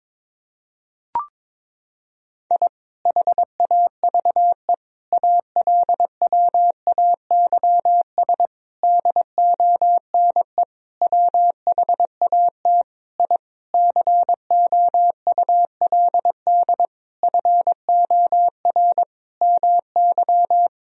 Morse code, heard as a keyed tone, about 22 wpm, 700 Hz. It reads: I HAVE ALWAYS DONE WHAT I COULD FOR MY
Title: Sentences from Top 1000 Words - Review Part 2 - 22wpm